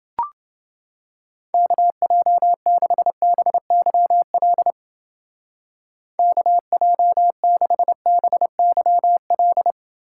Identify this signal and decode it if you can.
Morse code, heard as a keyed tone, about 28 wpm, 700 Hz. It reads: KJ6BYL KJ6BYL